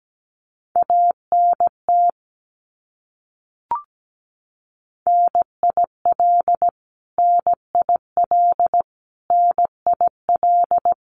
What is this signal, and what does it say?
Morse code, heard as a keyed tone, about 17 wpm, 700 Hz. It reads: ANT NIL NIL NIL
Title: Abbreviations Repeated 3x - 17wpm